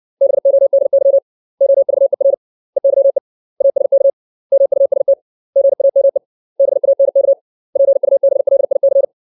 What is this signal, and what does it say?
Morse code, heard as a keyed tone, about 60 wpm, 550 Hz. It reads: BODY OVER EYE ASK MUST GAME BANK PUBLIC
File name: 632 words / 632 Words 60WPM